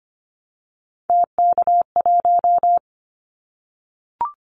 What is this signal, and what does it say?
Morse code, heard as a keyed tone, about 25 wpm, 700 Hz. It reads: TK1
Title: Callsign Prefixes for Top 100 DXCC - 25wpm